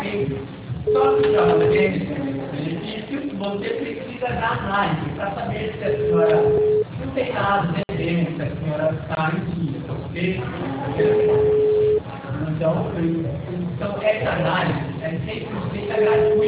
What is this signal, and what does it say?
Ringback tone (Europe, 425 Hz), with speech.